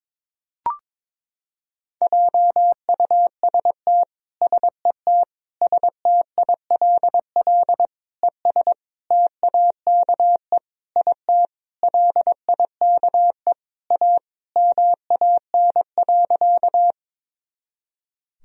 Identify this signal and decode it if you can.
Morse code, heard as a keyed tone, about 22 wpm, 700 Hz. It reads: JUST SET STILL ES TAKE IT LIKE A MAN.
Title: Sentences from Top 300 Words and Abbreviations 22wpm